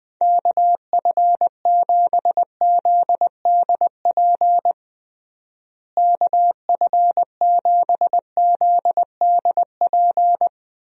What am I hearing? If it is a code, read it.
Morse code, heard as a keyed tone, about 21 wpm, 700 Hz. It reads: KF7ZDP KF7ZDP
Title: USA Call Signs - 6 Characters - Repeated 3x - 20wpm